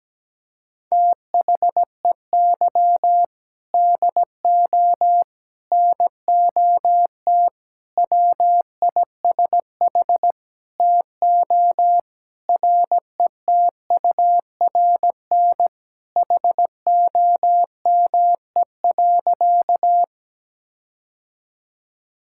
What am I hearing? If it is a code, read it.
Morse code, heard as a keyed tone, about 17 wpm, 700 Hz. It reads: THEY DO NOT WISH TO RETURN HOME.
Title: Sentences from Top 1600 Words 17wpm